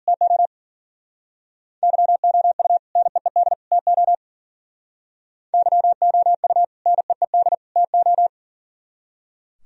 Morse code, heard as a keyed tone, about 51 wpm, 700 Hz. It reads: TO YOU NEED TO YOU NEED TO